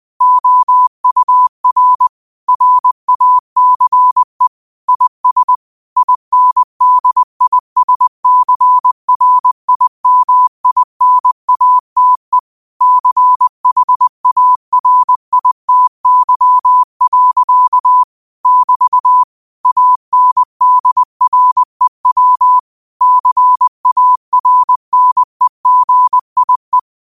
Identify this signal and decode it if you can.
Morse code, heard as a keyed tone, about 20 wpm, 1000 Hz. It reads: OUR RACE IS INDISCRIMINATE CHARITY. <BT> ANDREW CARNEGIE
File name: qotdfiles / 20wpm